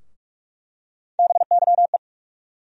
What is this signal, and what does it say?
Morse code, heard as a keyed tone, about 47 wpm, 700 Hz. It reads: BYE
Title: USA Call Sign Postfixes - 3 Letters 45wpm